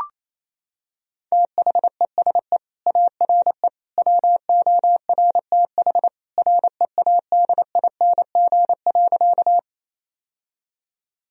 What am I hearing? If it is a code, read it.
Morse code, heard as a keyed tone, about 27 wpm, 690 Hz. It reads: THESE ARE WORTH READING.
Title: Sentences from Top 1000 Words - Review Part 3 - 28wpm